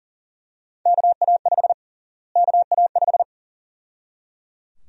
Morse code, heard as a keyed tone, about 37 wpm, 700 Hz. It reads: KA5 KA5
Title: USA Call Sign Prefixes - Repeated 2x - 40wpm